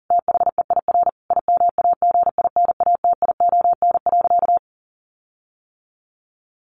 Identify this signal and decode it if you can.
Morse code, heard as a keyed tone, about 40 wpm, 700 Hz. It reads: THEIR IMAGINATION.
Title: Sentences from Top 1300 Words 40wpm